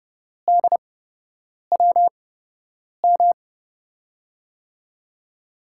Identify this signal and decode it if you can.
Morse code, heard as a keyed tone, about 30 wpm, spaced as for 8 wpm, 700 Hz. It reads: DWM
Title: USA Call Signs - 5 Characters - 8x Character Spacing - 30wpm